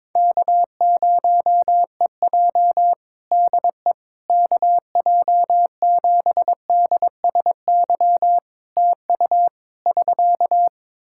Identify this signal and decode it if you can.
Morse code, heard as a keyed tone, about 22 wpm, 700 Hz. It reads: K0EJ DE KJ7DSY TU <SK>
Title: Practice QSO - Exchange 4 - Ending - 22wpm